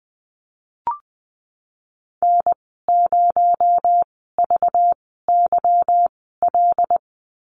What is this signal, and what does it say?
Morse code, heard as a keyed tone, about 19 wpm, spaced as for 14 wpm, 700 Hz. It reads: N0VYL
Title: 219 Lesson 35g - Number 0 - Call Signs 20wpm